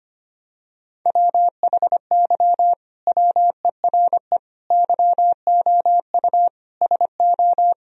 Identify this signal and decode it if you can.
Morse code, heard as a keyed tone, about 25 wpm, 700 Hz. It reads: WHY WERE YOU SO
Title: Sentences from Top 1000 Words - Review Part 4 - 25wpm